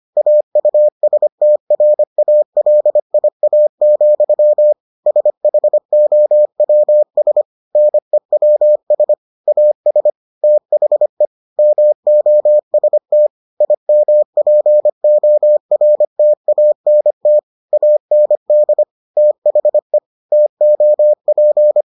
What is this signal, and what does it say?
Morse code, heard as a keyed tone, about 25 wpm, 600 Hz. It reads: AUSTRALIA, SHOWS NEWS AS THE MOST IMPORTANT AND THE TOP